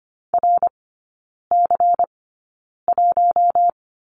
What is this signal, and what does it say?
Morse code, heard as a keyed tone, about 25 wpm, spaced as for 9 wpm, 700 Hz. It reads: RC1
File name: course / 25:10wpm / 093 Lesson 16e - Letter c - Three Characters 25wpm